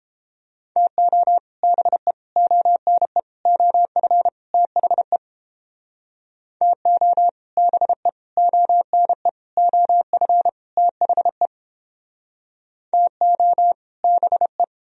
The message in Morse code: TO BE ONE OF THE TO BE ONE OF THE TO BE

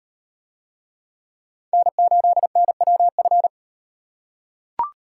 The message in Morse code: N8NWF

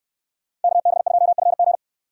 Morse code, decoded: KB2FG

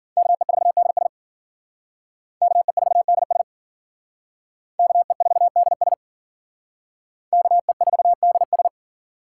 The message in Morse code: KE4DS KE4DS KE4DS KE4DS

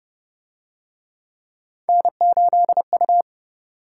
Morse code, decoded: N8U

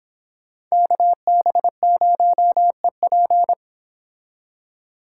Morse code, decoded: KB0EP

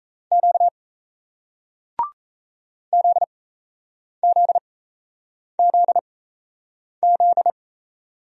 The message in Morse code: Q ZZZZ

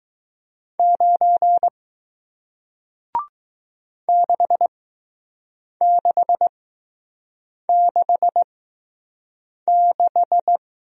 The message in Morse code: 9 6666

